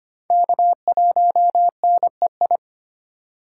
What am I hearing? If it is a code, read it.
Morse code, heard as a keyed tone, about 34 wpm, spaced as for 30 wpm, 700 Hz. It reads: K1NEI
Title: Speed Racing - USA Call Signs 25wpm